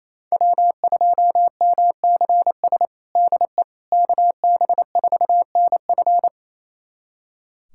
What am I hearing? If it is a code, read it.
Morse code, heard as a keyed tone, about 28 wpm, 700 Hz. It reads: W2MCS DE KB4NF
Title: USA Callsign DE Callsign - 5 Characters - 28wpm